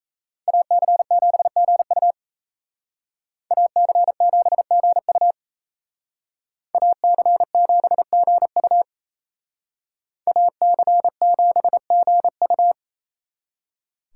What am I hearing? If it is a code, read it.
Morse code, heard as a keyed tone, about 33 wpm, 700 Hz. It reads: AC7GU AC7GU AC7GU AC7GU
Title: Speed Racing - USA Call Signs 28wpm